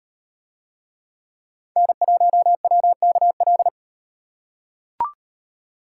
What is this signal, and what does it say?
Morse code, heard as a keyed tone, about 38 wpm, 700 Hz. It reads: N1WKL